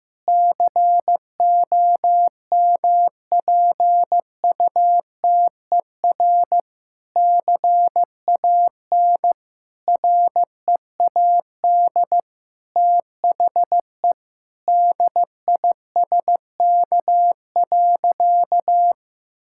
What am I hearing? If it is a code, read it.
Morse code, heard as a keyed tone, about 16 wpm, 700 Hz. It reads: COMPUTER CAN READ THE DISK.
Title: Sentences from Top 1000 Words - Review Part 6 - 15wpm